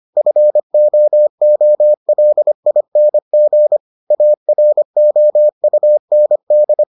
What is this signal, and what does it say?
Morse code, heard as a keyed tone, about 25 wpm, 600 Hz. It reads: FOOLING AROUND